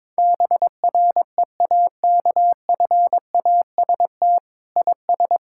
Morse code, heard as a keyed tone, about 22 wpm, 700 Hz. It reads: BREAKFAST IS
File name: Sets of 9 Words - 22wpm